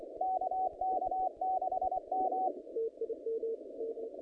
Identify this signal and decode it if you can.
Morse code, heard as a keyed tone, about 24 wpm, 750 Hz, with noise. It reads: KK6M
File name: contests / naqp-cw-k5tr-op-k5pi- -0100